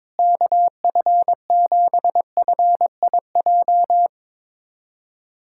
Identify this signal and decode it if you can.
Morse code, heard as a keyed tone, about 22 wpm, 700 Hz. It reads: KF7FIJ
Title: USA Callsign DE Callsign - 6 Characters - 22wpm